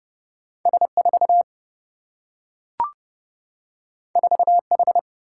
Morse code, heard as a keyed tone, about 30 wpm, 700 Hz. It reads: S 4 4 H
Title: Single-Letter Single-Number 30wpm